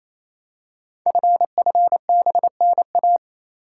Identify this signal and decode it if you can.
Morse code, heard as a keyed tone, about 27 wpm, 700 Hz. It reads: FFBNA
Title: Code Groups - 5 Letters - 28wpm